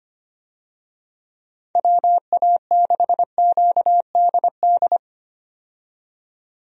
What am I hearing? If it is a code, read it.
Morse code, heard as a keyed tone, about 25 wpm, 700 Hz. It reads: WA6QDD
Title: USA Call Signs - 6 Characters - 25wpm